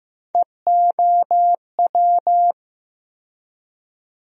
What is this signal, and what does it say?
Morse code, heard as a keyed tone, about 15 wpm, 700 Hz. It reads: EOW
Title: USA Call Sign Postfixes - 3 Letters 15wpm